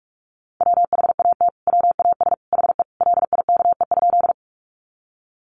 Morse code, heard as a keyed tone, about 44 wpm, 700 Hz. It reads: WHAT WAS HE LIKE?
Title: Sentences from Top 100 Words 45wpm